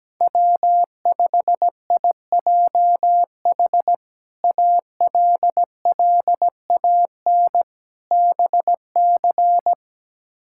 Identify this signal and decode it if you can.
Morse code, heard as a keyed tone, about 17 wpm, 700 Hz. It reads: W5IJH ALLAN BC
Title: K1USN SST QSO Practice - 17wpm